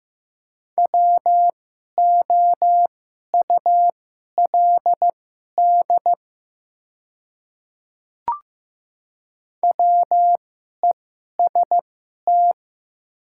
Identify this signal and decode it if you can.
Morse code, heard as a keyed tone, about 15 wpm, spaced as for 11 wpm, 700 Hz. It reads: WOULD WEST
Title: 2x Character Spacing - Top 500 Words - 15wpm